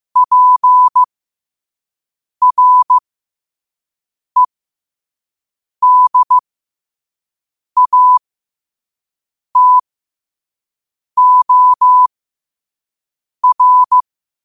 Morse code, heard as a keyed tone, about 15 wpm, spaced as for 5 wpm, 1000 Hz. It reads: PREDATOR
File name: qotdfiles / 5wpm